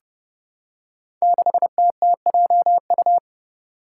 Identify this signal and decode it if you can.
Morse code, heard as a keyed tone, about 28 wpm, 700 Hz. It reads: 6TTJU
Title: Five Challenging Letter-Number Combinations 30wpm